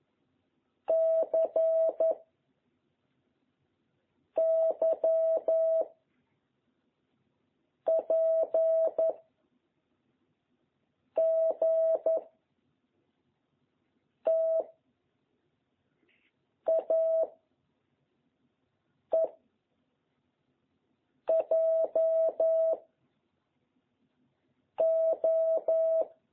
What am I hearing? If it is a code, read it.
Morse code, heard as a keyed tone, about 11 wpm, 660 Hz. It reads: C Y P G T A E J O
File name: jh8cbh / 2021cw